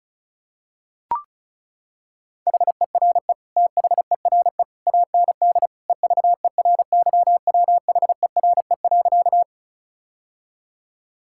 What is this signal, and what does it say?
Morse code, heard as a keyed tone, about 36 wpm, 700 Hz. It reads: HERE THERE AND EVERYWHERE.
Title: Sentences from Top 1600 Words 35wpm